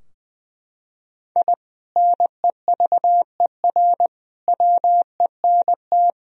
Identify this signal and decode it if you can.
Morse code, heard as a keyed tone, about 21 wpm, 700 Hz. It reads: I NEVER WENT
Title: Sentences from Top 1000 Words - Review Part 3 - 20wpm